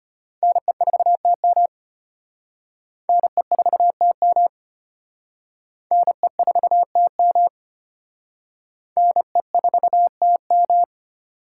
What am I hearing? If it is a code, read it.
Morse code, heard as a keyed tone, about 34 wpm, 700 Hz. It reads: NE4TM NE4TM NE4TM NE4TM